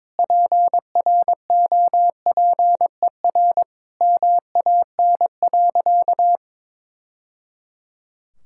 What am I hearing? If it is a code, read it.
Morse code, heard as a keyed tone, about 22 wpm, 700 Hz. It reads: PROPER MAN.
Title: Sentences from Top 1000 Words - Review Part 5 - 22wpm